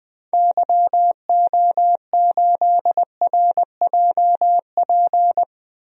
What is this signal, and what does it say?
Morse code, heard as a keyed tone, about 20 wpm, 700 Hz. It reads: YO8RJP
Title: DXCC-100 Call Signs - 6 Characters - 20wpm